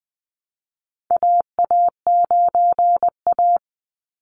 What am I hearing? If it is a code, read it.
Morse code, heard as a keyed tone, about 20 wpm, 700 Hz. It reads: AA9A
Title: Call Signs - 20wpm